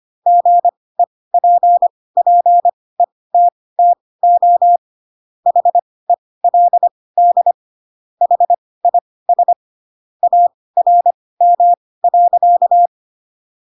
Morse code, heard as a keyed tone, about 25 wpm, spaced as for 18 wpm, 700 Hz. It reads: GEPPETTO HELD HIS ARM.
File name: Pinocchio / Pinocchio ch7 12 25 ik1hge